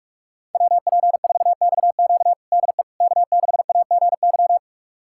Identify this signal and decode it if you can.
Morse code, heard as a keyed tone, about 42 wpm, 700 Hz. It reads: WP4XQ DE K6AGY
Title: USA Callsign DE Callsign 45wpm